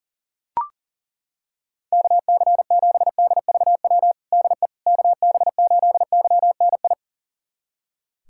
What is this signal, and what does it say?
Morse code, heard as a keyed tone, about 40 wpm, 700 Hz. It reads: KC7DVW DE KB8YNI
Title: USA Callsign DE Callsign 40wpm